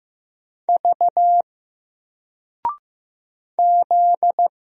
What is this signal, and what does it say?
Morse code, heard as a keyed tone, about 15 wpm, 700 Hz. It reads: V Z
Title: Single Character (Letter:Number:Punctuation:Prosign) 15wpm